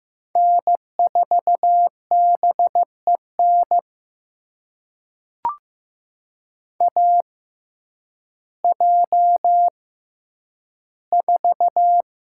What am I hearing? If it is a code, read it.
Morse code, heard as a keyed tone, about 15 wpm, spaced as for 9 wpm, 700 Hz. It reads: N4BEN AJ4
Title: USA Call Signs - 6x Character Spacing - 15wpm